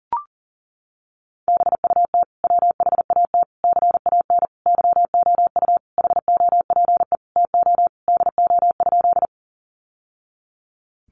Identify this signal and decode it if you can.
Morse code, heard as a keyed tone, about 40 wpm, 700 Hz. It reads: BUT WHAT CAN YOU HOPE TO DO?
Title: Sentences from Top 1000 Words 40wpm